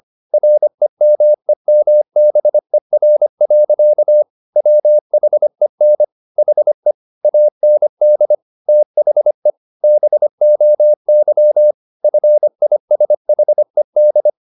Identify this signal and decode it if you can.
Morse code, heard as a keyed tone, about 25 wpm, 600 Hz. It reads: REMEMBER. WHEN HE AND THE BOY FISHED
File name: the old man and the sea part 1 25wpm